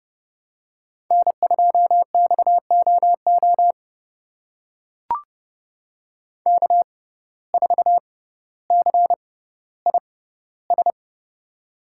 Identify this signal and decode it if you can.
Morse code, heard as a keyed tone, about 30 wpm, spaced as for 15 wpm, 700 Hz. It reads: N2XOO K4CIS
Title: USA Call Signs - 5 Characters - 6x Character Spacing - 30wpm